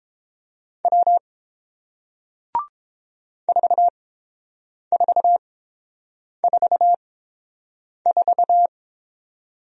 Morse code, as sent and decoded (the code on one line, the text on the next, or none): .-- / ....- ....- ....- ....-
W 4444